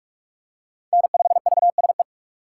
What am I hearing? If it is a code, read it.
Morse code, heard as a keyed tone, about 46 wpm, 700 Hz. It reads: N5VSE
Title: USA Call Signs - 4x Character Spacing - 45wpm